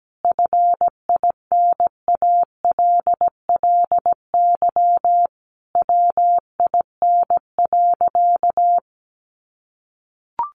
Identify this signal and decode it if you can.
Morse code, heard as a keyed tone, about 17 wpm, 700 Hz. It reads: FINALLY WIN.
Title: Sentences from Top 1500 Words 17wpm